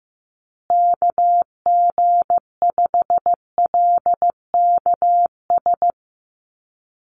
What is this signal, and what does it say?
Morse code, heard as a keyed tone, about 15 wpm, 700 Hz. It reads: KG5LKS
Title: USA Call Signs Repeated 3x 15wpm